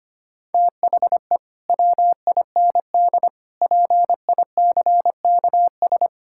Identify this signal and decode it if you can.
Morse code, heard as a keyed tone, about 25 wpm, 700 Hz. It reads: THE WIND PICKS
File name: practice / 25wpm / Difficult Sentences with Hints - 25wpm